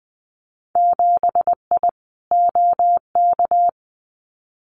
Morse code, heard as a keyed tone, about 20 wpm, 700 Hz. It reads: 7I OK
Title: Winter Field Day Exchanges Encore 20wpm